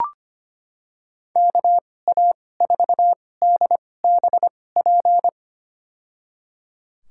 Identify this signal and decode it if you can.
Morse code, heard as a keyed tone, about 25 wpm, spaced as for 18 wpm, 700 Hz. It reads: KA4DBP
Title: USA Call Signs - 6 Characters - 2x Character Spacing - 25wpm